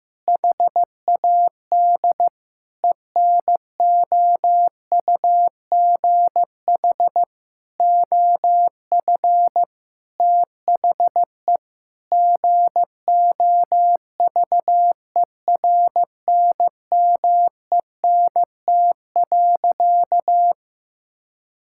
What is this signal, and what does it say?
Morse code, heard as a keyed tone, about 15 wpm, 700 Hz. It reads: HAD ENOUGH OF THE GOVERNMENT.
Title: Sentences from Top 1600 Words 15wpm